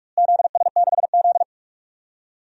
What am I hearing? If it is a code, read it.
Morse code, heard as a keyed tone, about 47 wpm, 700 Hz. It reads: ZS6Z
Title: DXCC-100 Call Signs - 4 Characters - Repeated 3x - 45wpm